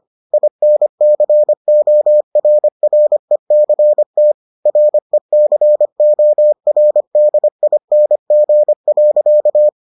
Morse code, heard as a keyed tone, about 25 wpm, 600 Hz. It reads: INCORRECT RECORDING.